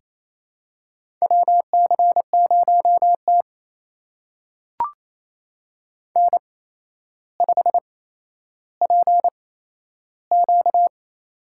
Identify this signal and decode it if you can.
Morse code, heard as a keyed tone, about 28 wpm, spaced as for 12 wpm, 700 Hz. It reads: WC0T N5PQ
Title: USA Call Signs - 4 Characters - 8x Character Spacing - 28wpm